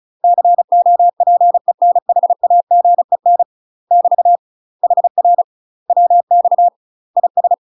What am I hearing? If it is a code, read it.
Morse code, heard as a keyed tone, about 34 wpm, 700 Hz. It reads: COPENHAGEN <BT> HR WX IS
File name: UK MASTER QSOs TRAINING / 35WPM / UK MASTER QSOs 0040